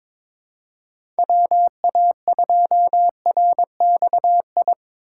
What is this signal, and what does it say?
Morse code, heard as a keyed tone, about 21 wpm, 700 Hz. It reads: WA2RXI
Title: USA Call Signs - 6 Characters - 3x Character Spacing - 22wpm